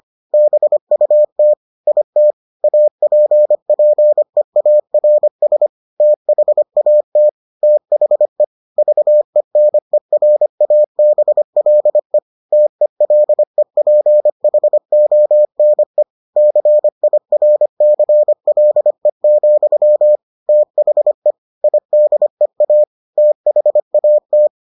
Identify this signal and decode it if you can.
Morse code, heard as a keyed tone, about 25 wpm, 600 Hz. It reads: BUT IT APPEARS THAT THE VENERABLE TELEPHONE CIRCLE, THE IDEA THAT